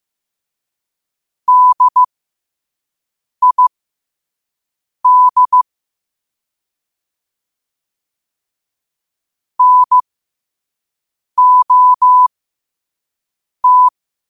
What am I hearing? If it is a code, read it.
Morse code, heard as a keyed tone, about 15 wpm, spaced as for 5 wpm, 1000 Hz. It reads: DID NOT
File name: qotdfiles / 5wpm